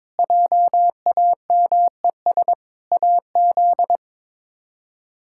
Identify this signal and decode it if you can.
Morse code, heard as a keyed tone, about 21 wpm, 700 Hz. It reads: JAMES AZ
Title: K1USN SST QSO Practice - 22wpm